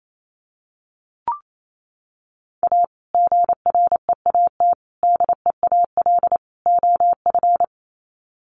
Morse code, heard as a keyed tone, about 28 wpm, 700 Hz. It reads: A GREAT DEAL OF
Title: Sets of 4 Words 28wpm